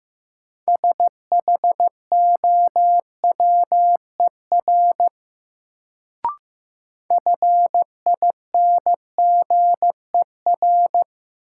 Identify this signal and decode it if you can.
Morse code, heard as a keyed tone, about 15 wpm, 700 Hz. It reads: SHOWER FINGER
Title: Sentences from Top 1600 Words - New Word Drill - 15wpm